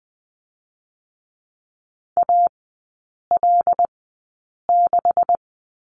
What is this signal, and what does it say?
Morse code, heard as a keyed tone, about 19 wpm, spaced as for 8 wpm, 700 Hz. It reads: AL6
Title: 127 Lesson 21e - Number 6 - Three Characters 20wpm